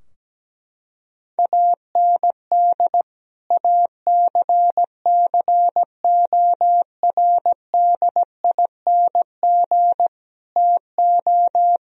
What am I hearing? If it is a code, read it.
Morse code, heard as a keyed tone, about 17 wpm, 700 Hz. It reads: AND ACCORDING TO
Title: Sentences from Top 1000 Words - Review Part 4 - 17wpm